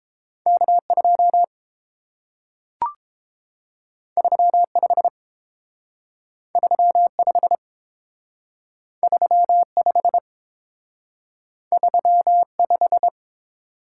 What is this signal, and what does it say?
Morse code, heard as a keyed tone, about 30 wpm, 700 Hz. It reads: K2 35 35 35 35